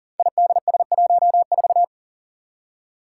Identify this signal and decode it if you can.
Morse code, heard as a keyed tone, about 38 wpm, 700 Hz. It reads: IDS14
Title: Five Challenging Letter-Number Combinations 40wpm